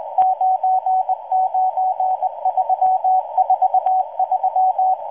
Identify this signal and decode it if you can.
Morse code, heard as a keyed tone, about 19 wpm, 750 Hz, with radio noise. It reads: 99343